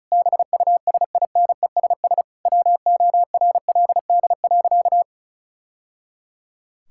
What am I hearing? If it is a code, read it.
Morse code, heard as a keyed tone, about 34 wpm, 700 Hz. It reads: BUSINESS WORLD.
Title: Sentences from Top 1000 Words - Review Part 6 - 35wpm